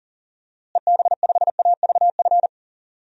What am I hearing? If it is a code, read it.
Morse code, heard as a keyed tone, about 40 wpm, 700 Hz. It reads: EB5AVF
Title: DXCC-100 Call Signs - 6 Characters - Repeated 2x - 40wpm